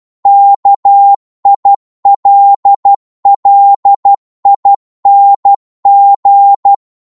Morse code, heard as a keyed tone, about 12 wpm, 800 Hz. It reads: KILLING
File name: cwmp3 / psg-1-12wpm